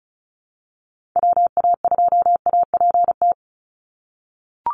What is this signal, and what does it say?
Morse code, heard as a keyed tone, about 34 wpm, 700 Hz. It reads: WA2APT
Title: USA Call Signs 35wpm